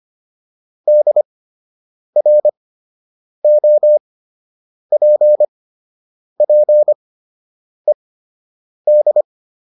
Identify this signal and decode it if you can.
Morse code, heard as a keyed tone, about 25 wpm, spaced as for 8 wpm, 600 Hz. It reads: DROPPED